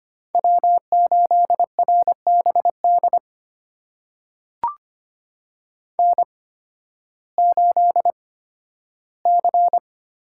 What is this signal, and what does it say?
Morse code, heard as a keyed tone, about 25 wpm, spaced as for 13 wpm, 700 Hz. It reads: W8RBD N8C
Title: USA Call Signs - 5 Characters - 8x Character Spacing - 25wpm